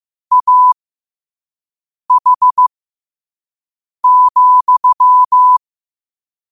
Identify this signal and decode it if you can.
Morse code, heard as a keyed tone, about 15 wpm, spaced as for 5 wpm, 1000 Hz. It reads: AH,
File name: qotdfiles / 5wpm